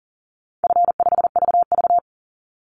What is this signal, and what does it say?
Morse code, heard as a keyed tone, about 41 wpm, 700 Hz. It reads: F5VV
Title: Call Signs - 40wpm